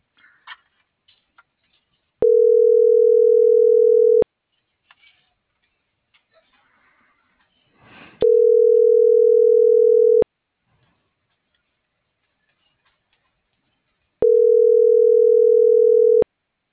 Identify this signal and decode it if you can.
Ringback tone (North America), 440 + 480 Hz.